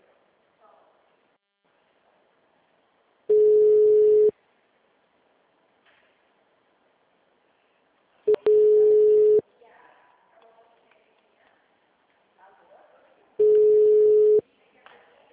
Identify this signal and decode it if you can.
Ringback tone (Europe, 425 Hz), with speech.
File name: out- -2024- - - .4781